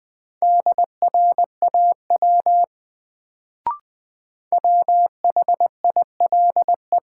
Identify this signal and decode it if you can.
Morse code, heard as a keyed tone, about 21 wpm, 700 Hz. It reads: DRAW WHILE
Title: Sentences from Top 300 Words - New Word Drill - 20wpm